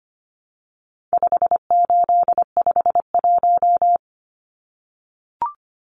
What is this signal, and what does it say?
Morse code, heard as a keyed tone, about 25 wpm, 700 Hz. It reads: 5851
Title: Numbers - 4-Digit 25wpm